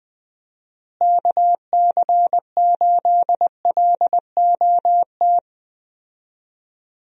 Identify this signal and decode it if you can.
Morse code, heard as a keyed tone, about 20 wpm, 700 Hz. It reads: KC8LOT
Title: USA Call Signs - 6 Characters - Repeated 2x - 20wpm